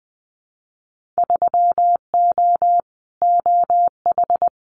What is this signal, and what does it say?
Morse code, heard as a keyed tone, about 20 wpm, 700 Hz. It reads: 3O OH
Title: Winter Field Day Exchanges 20wpm